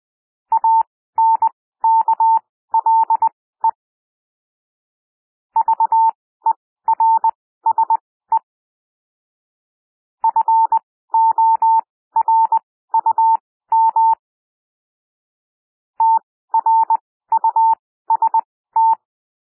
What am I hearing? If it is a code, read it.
Morse code, heard as a keyed tone, about 20 wpm, spaced as for 14 wpm, 900 Hz. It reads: ANKLE VERSE FORUM TRUST
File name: SOUND WORDS 20 wpm